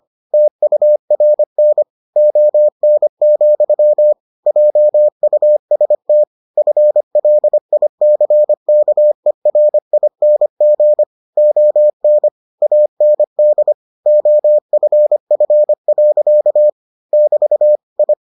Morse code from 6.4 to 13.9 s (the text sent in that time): FLICKERING ON AND